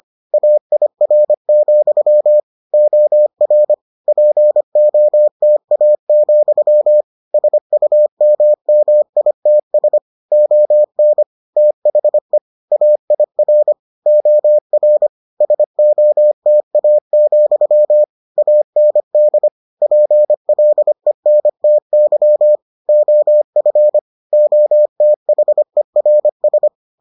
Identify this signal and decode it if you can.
Morse code, heard as a keyed tone, about 25 wpm, 600 Hz. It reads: AIR, OR POTA, SUMMITS ON THE AIR OR SOTA, AND PLENTY OF OTHERS